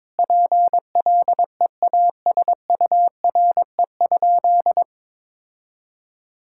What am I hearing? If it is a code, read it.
Morse code, heard as a keyed tone, about 22 wpm, 700 Hz. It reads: PLEASURE?
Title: Sentences from Top 1600 Words 22wpm